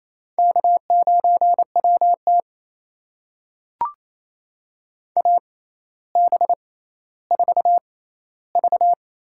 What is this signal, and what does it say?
Morse code, heard as a keyed tone, about 28 wpm, spaced as for 14 wpm, 700 Hz. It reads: K9WT AB4V
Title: USA Call Signs - 4 Characters - 6x Character Spacing - 28wpm